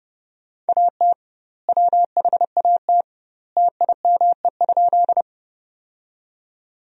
Morse code, heard as a keyed tone, about 31 wpm, 700 Hz. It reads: AT WHAT TIME?
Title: 2x Word-Spacing - Sentences from Top 100 Words - 30wpm